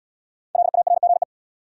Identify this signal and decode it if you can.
Morse code, heard as a keyed tone, about 69 wpm, 700 Hz. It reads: LARGE